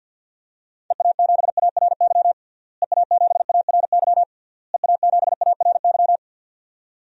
Morse code, heard as a keyed tone, about 50 wpm, 700 Hz. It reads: EA7ARY EA7ARY EA7ARY
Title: DXCC-100 Call Signs - 6 Characters - Repeated 3x - 50wpm